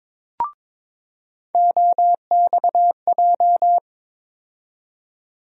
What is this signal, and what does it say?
Morse code, heard as a keyed tone, about 22 wpm, 700 Hz. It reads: OXJ
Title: USA Call Sign Postfixes - 3 Letters 22wpm